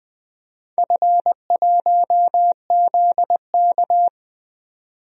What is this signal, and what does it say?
Morse code, heard as a keyed tone, about 21 wpm, 700 Hz. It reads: F1ZK
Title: DXCC-100 Call Signs - 4 Characters - 20wpm